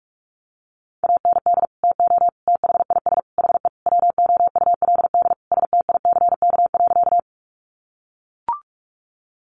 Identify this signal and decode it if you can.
Morse code, heard as a keyed tone, about 44 wpm, 700 Hz. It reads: AND TO THIS HE WOULD STICK.
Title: Sentences from Top 1200 Words 45wpm